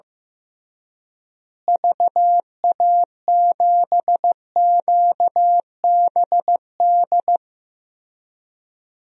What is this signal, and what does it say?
Morse code, heard as a keyed tone, about 15 wpm, 700 Hz. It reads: VA7QBD
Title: DXCC-100 Call Signs - 6 Characters - Repeated 3x - 15wpm